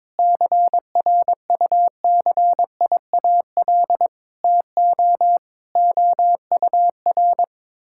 Morse code, heard as a keyed tone, about 22 wpm, 700 Hz. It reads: CRUCIAL TO OUR
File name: Sentences with 9-Word Phrases - 22wpm